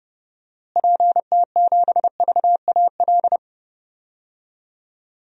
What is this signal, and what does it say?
Morse code, heard as a keyed tone, about 32 wpm, 700 Hz. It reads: PT7VAL
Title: DXCC-100 Call Signs - 6 Characters - Repeated 3x - 30wpm